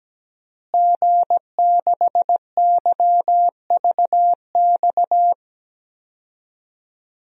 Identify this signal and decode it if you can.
Morse code, heard as a keyed tone, about 17 wpm, 700 Hz. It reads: G6YVX
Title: DXCC-100 Call Signs - 5 Characters - Repeated 3x - 17wpm